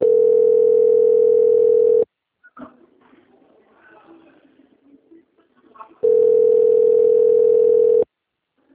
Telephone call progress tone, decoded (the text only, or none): ringback tone